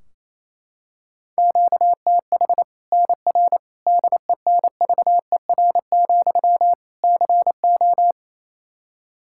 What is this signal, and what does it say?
Morse code, heard as a keyed tone, about 28 wpm, 700 Hz. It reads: QTH NR DENVER, CO
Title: K1USN SST QSO Practice 28wpm